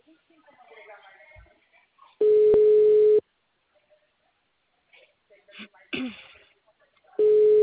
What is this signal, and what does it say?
Ringback tone (Europe, 425 Hz), with speech.
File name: out- -2023- - - .66109